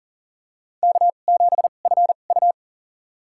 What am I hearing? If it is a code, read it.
Morse code, heard as a keyed tone, about 37 wpm, spaced as for 28 wpm, 700 Hz. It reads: K7FU
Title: USA Call Signs - 4 Characters - 2x Character Spacing - 40wpm